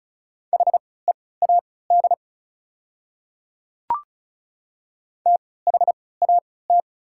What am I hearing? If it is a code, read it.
Morse code, heard as a keyed tone, about 34 wpm, spaced as for 20 wpm, 700 Hz. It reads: HEAD THAT
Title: 3x Character Spacing - Top 300 Words - 35wpm